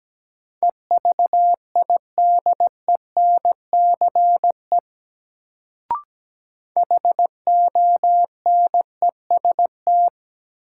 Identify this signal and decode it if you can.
Morse code, heard as a keyed tone, about 17 wpm, 700 Hz. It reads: EVIDENCE HONEST
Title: Sets of 5 Words Encore - New Word Drill - 17wpm